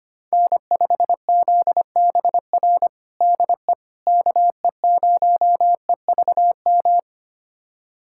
Morse code, heard as a keyed tone, about 25 wpm, 700 Hz. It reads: N5ZBR DE KE0EVM
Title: USA Callsign DE Callsign 25wpm